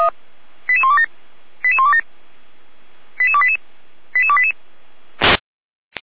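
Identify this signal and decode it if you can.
Five-tone selective call (ZVEI), heard as quick runs of five tones, tones of about 80 ms. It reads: 88127 88127 88288 88288
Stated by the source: Five-tone ZVEI selective call received on an SDR.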